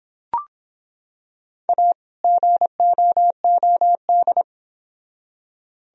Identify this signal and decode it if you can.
Morse code, heard as a keyed tone, about 25 wpm, 700 Hz. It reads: A GOOD